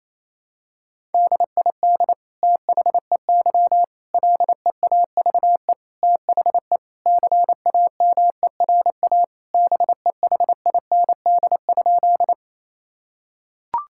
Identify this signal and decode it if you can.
Morse code, heard as a keyed tone, about 28 wpm, 700 Hz. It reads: DID THEY LEAVE THE CAMERA BEHIND?
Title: Sentences from Top 1000 Words - Review Part 5 - 28wpm